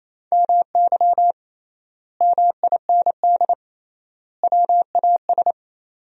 Morse code, heard as a keyed tone, about 28 wpm, 700 Hz. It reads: MY MIND WAS
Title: 3x Word-Spacing - Sentences from Top 500 Words - 28wpm